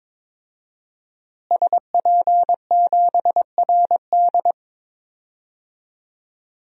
Morse code, heard as a keyed tone, about 22 wpm, 700 Hz. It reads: SP7RD
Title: DXCC-100 Call Signs - 5 Characters - Repeated 2x - 22wpm